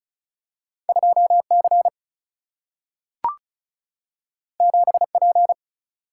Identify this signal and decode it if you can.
Morse code, heard as a keyed tone, about 36 wpm, 700 Hz. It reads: 2 C 7 P
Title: Single-Letter Single-Number 35wpm